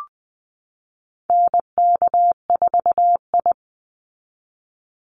Morse code, heard as a keyed tone, about 20 wpm, 700 Hz. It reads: NK4I
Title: Call Signs Repeated 3x - 20wpm